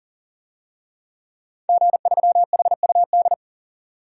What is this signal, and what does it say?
Morse code, heard as a keyed tone, about 40 wpm, 700 Hz. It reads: G3HUD
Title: DXCC-100 Call Signs - 5 Characters - 40wpm